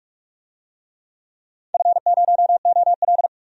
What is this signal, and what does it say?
Morse code, heard as a keyed tone, about 43 wpm, 700 Hz. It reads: F0OL
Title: DXCC-100 Call Signs - 4 Characters - 45wpm